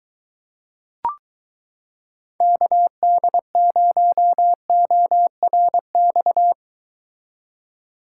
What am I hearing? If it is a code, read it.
Morse code, heard as a keyed tone, about 22 wpm, 700 Hz. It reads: KD0ORX